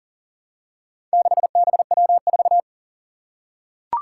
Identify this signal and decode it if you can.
Morse code, heard as a keyed tone, about 40 wpm, 700 Hz. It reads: 6BW4
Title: Four Challenging Letter-Number Combinations 40wpm